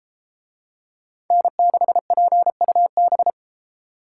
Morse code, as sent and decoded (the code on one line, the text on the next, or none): -. -.... .--. ..- -...
N6PUB